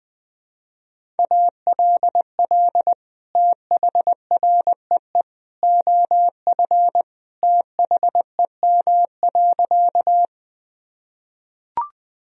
Morse code, heard as a keyed tone, about 21 wpm, 700 Hz. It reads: ALL THREE OF THEM.
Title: Sentences from Top 1000 Words - Review Part 1 - 20wpm